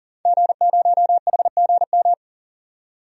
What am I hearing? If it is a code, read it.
Morse code, heard as a keyed tone, about 40 wpm, 700 Hz. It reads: G0HGM
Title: DXCC-100 Call Signs - 5 Characters - 40wpm